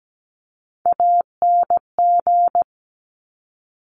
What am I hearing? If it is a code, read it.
Morse code, heard as a keyed tone, about 17 wpm, 700 Hz. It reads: ANG
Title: Top Three Letter Combinations in Words 17wpm